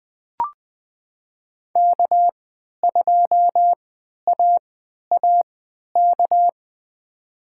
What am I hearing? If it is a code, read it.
Morse code, heard as a keyed tone, about 20 wpm, spaced as for 11 wpm, 700 Hz. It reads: K2AAK
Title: USA Call Signs - 5 Characters - 3x Character Spacing - 20wpm